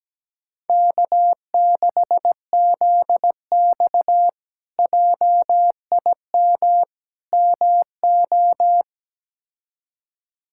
Morse code, heard as a keyed tone, about 17 wpm, 700 Hz. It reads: K6ZX JIM MO
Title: K1USN SST QSO Practice - 17wpm